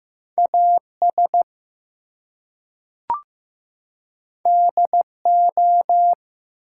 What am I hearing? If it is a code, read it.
Morse code, heard as a keyed tone, about 15 wpm, 700 Hz. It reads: AS DO
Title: Two Letter Words - 15wpm